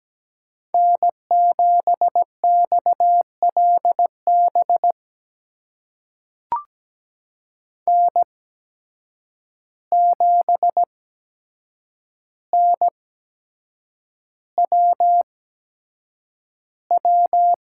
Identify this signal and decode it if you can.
Morse code, heard as a keyed tone, about 17 wpm, spaced as for 7 wpm, 700 Hz. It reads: N7XLB N7NWW
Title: USA Call Signs - 5 Characters - 8x Character Spacing - 17wpm